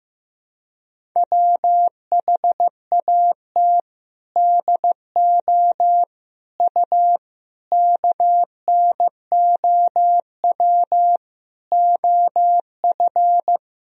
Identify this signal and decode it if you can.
Morse code, heard as a keyed tone, about 15 wpm, 700 Hz. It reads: WHAT DO U KNOW OF
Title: Sentences from Top 100 Words and Abbreviations 15wpm